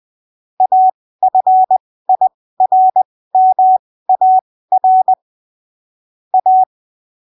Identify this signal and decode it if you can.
Morse code, heard as a keyed tone, about 20 wpm, spaced as for 15 wpm, 750 Hz. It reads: AFIRMAR A
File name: arquivos / texto / texto 24 20ppm